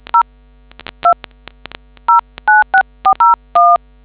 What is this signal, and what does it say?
DTMF, heard as telephone touch tones, tones of about 120 ms. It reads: 0 2 096401